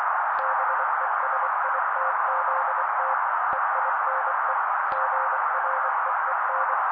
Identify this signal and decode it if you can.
Morse code, heard as a keyed tone, about 24 wpm, 530 Hz. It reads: BESITZT EINE GREEN
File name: daily / 23wpm noise